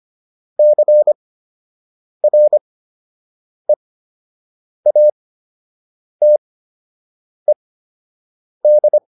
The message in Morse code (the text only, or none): CREATED